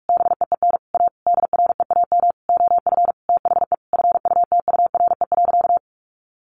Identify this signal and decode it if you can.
Morse code, heard as a keyed tone, about 44 wpm, 700 Hz. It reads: BEEN A DREAM OF THE FUTURE.
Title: Sentences from Top 1500 Words 45wpm